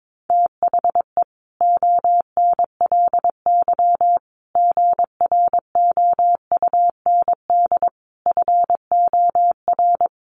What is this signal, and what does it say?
Morse code, heard as a keyed tone, about 22 wpm, 700 Hz. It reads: THE ONLY GROUND FOR
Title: Sentences from Top 1400 Words 22wpm